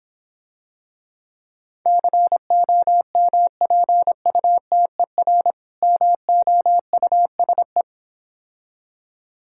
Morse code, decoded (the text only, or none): COMPUTER MOUSE